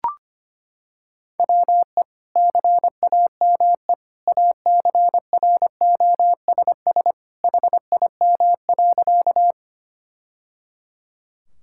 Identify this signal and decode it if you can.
Morse code, heard as a keyed tone, about 25 wpm, 700 Hz. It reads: WE CAME ACROSS HIM.
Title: Sentences from Top 1000 Words - Review Part 5 - 25wpm